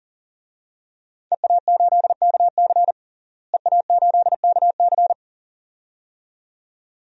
Morse code, heard as a keyed tone, about 40 wpm, 700 Hz. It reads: EA8KC EA8KC
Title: DXCC-100 Call Signs - 5 Characters - Repeated 2x - 40wpm